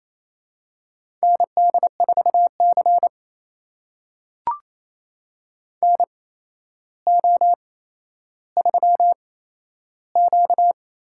Morse code, heard as a keyed tone, about 28 wpm, spaced as for 12 wpm, 700 Hz. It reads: ND4C NO3Q
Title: USA Call Signs - 4 Characters - 8x Character Spacing - 28wpm